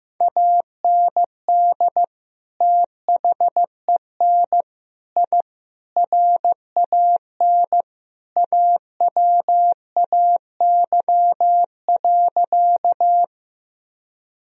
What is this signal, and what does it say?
Morse code, heard as a keyed tone, about 15 wpm, 700 Hz. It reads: AND THEN I RAN AWAY.
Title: Sentences from Top 1000 Words - Review Part 3 - 15wpm